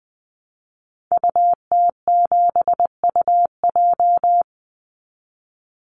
Morse code, heard as a keyed tone, about 20 wpm, 700 Hz. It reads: UT7UJ
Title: Call Signs Repeated 2x - 20wpm